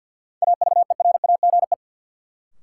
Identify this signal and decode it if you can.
Morse code, heard as a keyed tone, about 53 wpm, 700 Hz. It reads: AVERAGE
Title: Sentences from Top 1000 Words - Review Part 6 - New Word Drill - 50wpm